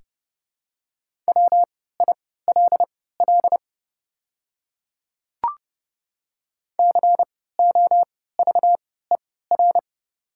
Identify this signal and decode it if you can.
Morse code, heard as a keyed tone, about 30 wpm, spaced as for 17 wpm, 690 Hz. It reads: WILL COVER
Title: 3x Character Spacing - Top 500 Words - 30wpm